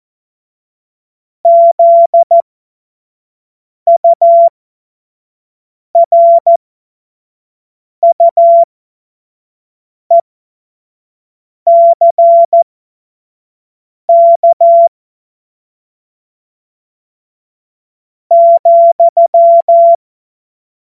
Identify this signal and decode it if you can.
Morse code, heard as a keyed tone, about 14 wpm, spaced as for 4 wpm, 680 Hz. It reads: ZURUECK,
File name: Morsekurs / Witze / Witz 01. März 2023 25 F70